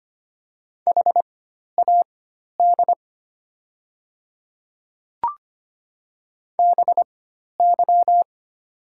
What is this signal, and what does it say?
Morse code, heard as a keyed tone, about 25 wpm, spaced as for 12 wpm, 700 Hz. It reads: HAD BY
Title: 4x Character Spacing - Top 100 Words - 25wpm